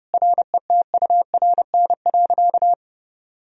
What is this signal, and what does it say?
Morse code, heard as a keyed tone, about 30 wpm, 700 Hz. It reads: RETURN.
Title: Sentences from Top 1000 Words - Review Part 1 - 30wpm